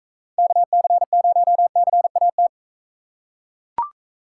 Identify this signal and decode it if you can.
Morse code, heard as a keyed tone, about 42 wpm, 700 Hz. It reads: KC0CAT